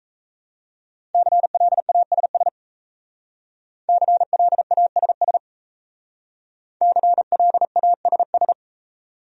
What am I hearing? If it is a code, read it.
Morse code, heard as a keyed tone, about 37 wpm, 700 Hz. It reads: CLASS CLASS CLASS